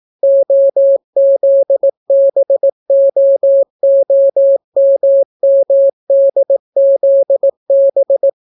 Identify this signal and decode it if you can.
Morse code, heard as a keyed tone, about 18 wpm, 550 Hz. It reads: OZBOOMMDZB